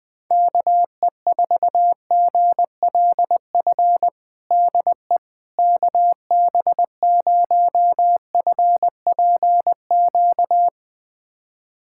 Morse code, heard as a keyed tone, about 21 wpm, 700 Hz. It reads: KE4GLF DE KB0FPQ
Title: USA Callsign DE Callsign 20wpm